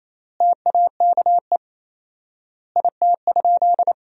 Morse code, heard as a keyed tone, about 28 wpm, 700 Hz. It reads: TAKE IT?
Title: 4x Word-Spacing - Sentences from Top 200 Words - 28wpm